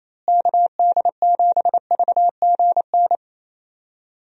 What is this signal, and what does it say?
Morse code, heard as a keyed tone, about 28 wpm, 700 Hz. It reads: KD7VGN
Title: USA Call Signs - 6 Characters - Repeated 2x - 28wpm